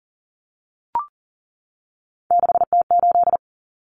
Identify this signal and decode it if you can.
Morse code, heard as a keyed tone, about 40 wpm, 700 Hz. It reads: 6T8
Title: Common Cut Numbers - 3-Digit 40wpm